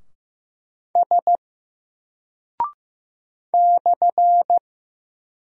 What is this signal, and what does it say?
Morse code, heard as a keyed tone, about 15 wpm, 700 Hz. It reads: S /
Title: Single Character (Letter:Number:Punctuation:Prosign) 15wpm